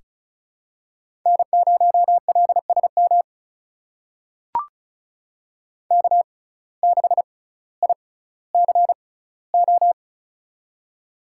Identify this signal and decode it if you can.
Morse code, heard as a keyed tone, about 36 wpm, spaced as for 18 wpm, 700 Hz. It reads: N0LSM K6ICO
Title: USA Call Signs - 6x Character Spacing - 35wpm